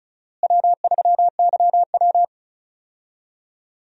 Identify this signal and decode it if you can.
Morse code, heard as a keyed tone, about 36 wpm, 700 Hz. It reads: W3YW
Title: USA Call Signs - 4 Characters - Repeated 3x - 35wpm